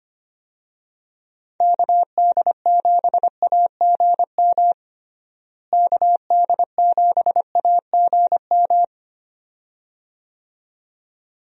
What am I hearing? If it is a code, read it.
Morse code, heard as a keyed tone, about 25 wpm, 700 Hz. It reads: KD7AGM KD7AGM
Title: USA Call Signs - 6 Characters - Repeated 2x - 25wpm